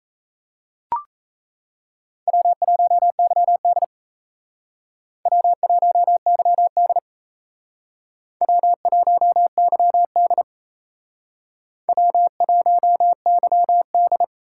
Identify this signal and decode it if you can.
Morse code, heard as a keyed tone, about 34 wpm, 700 Hz. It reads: W1YD W1YD W1YD W1YD
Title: Speed Racing - USA Call Signs 28wpm